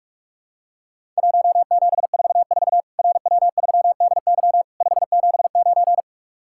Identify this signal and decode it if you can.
Morse code, heard as a keyed tone, about 44 wpm, 700 Hz. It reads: 1744 RW3DY 579
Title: Worked All Europe DX Contest - QTC Exchange - 45wpm